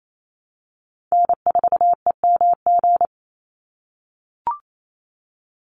Morse code, heard as a keyed tone, about 28 wpm, 700 Hz. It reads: N4EMG
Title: NA SOTA Chaser Callsigns 28wpm